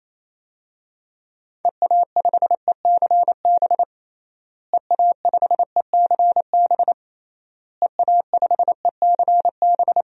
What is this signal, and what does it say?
Morse code, heard as a keyed tone, about 28 wpm, 700 Hz. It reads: EA5ECB EA5ECB EA5ECB
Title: DXCC-100 Call Signs - 6 Characters - Repeated 3x - 28wpm